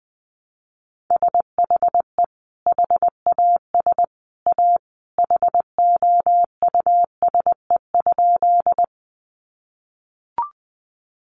Morse code, heard as a keyed tone, about 19 wpm, 700 Hz. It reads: SHE HAS A HOUSE?
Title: Sentences from Top 200 Words 20wpm